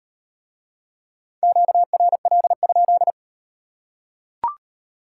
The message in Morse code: QRL?

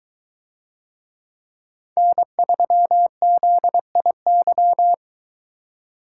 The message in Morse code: N3ZIY

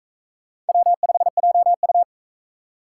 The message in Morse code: W5JU

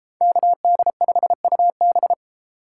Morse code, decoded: KD5UB

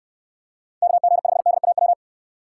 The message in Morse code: KG5DAW

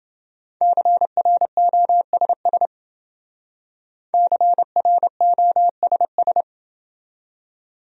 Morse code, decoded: CROSS CROSS

